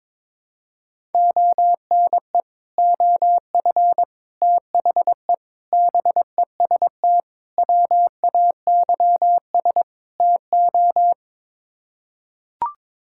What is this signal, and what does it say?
Morse code, heard as a keyed tone, about 22 wpm, 700 Hz. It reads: ONE OF THE BEST WAYS TO
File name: practice / 22wpm / Sets of 6 Words - 22wpm